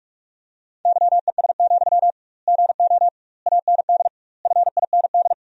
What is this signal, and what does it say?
Morse code, heard as a keyed tone, about 47 wpm, 700 Hz. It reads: YES, GO AND FIND
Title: 2x Word-Spacing - Sentences from Top 500 Words - 45wpm